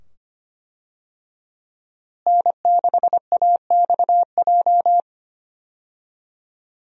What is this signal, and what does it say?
Morse code, heard as a keyed tone, about 25 wpm, 690 Hz. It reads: N6AXJ
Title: USA Call Signs - 5 Characters - Repeated 2x - 25wpm